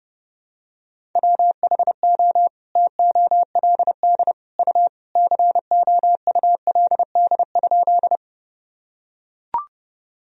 Morse code, heard as a keyed tone, about 30 wpm, 700 Hz. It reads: WHO TOLD U COULD?
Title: Sentences from Top 400 Words and Abbreviations 30wpm